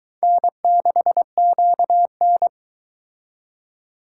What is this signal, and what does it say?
Morse code, heard as a keyed tone, about 22 wpm, 700 Hz. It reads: N6QN